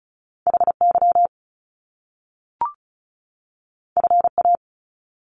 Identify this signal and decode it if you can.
Morse code, heard as a keyed tone, about 36 wpm, 700 Hz. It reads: HY FA
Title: Top Two Letter Combinations in Words 35wpm